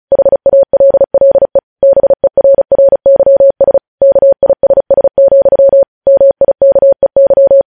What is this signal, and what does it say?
Morse code, heard as a keyed tone, about 36 wpm, 550 Hz. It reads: HALLE BERRYS KISS, MIKEY
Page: News Headlines in Morse Code at 35 WPM